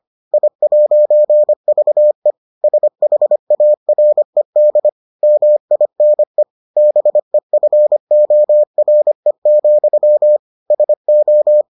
Morse code, heard as a keyed tone, about 25 wpm, 600 Hz. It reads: I'VE SHARED MINE BEFORE, SO